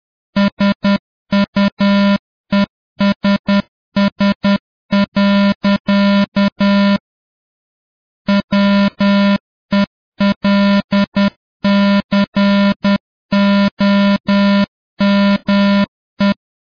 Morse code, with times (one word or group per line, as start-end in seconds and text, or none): 0.4-7.0 SUESS.
8.3-16.3 WELCOME